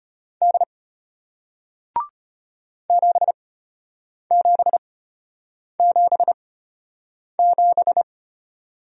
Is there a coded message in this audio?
Yes: Morse code from 0.4 to 8.0 s, heard as a keyed tone, about 33 wpm, spaced as for 8 wpm, 700 Hz. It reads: D 7777